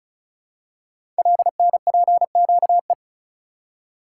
Morse code, heard as a keyed tone, about 36 wpm, 700 Hz. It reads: LNPQE
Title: Code Groups - 5 Letters - 35wpm